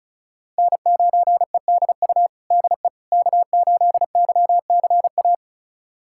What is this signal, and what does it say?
Morse code, heard as a keyed tone, about 36 wpm, 700 Hz. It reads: N9EDU DE K8YCA
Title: USA Callsign DE Callsign - 5 Characters - 35wpm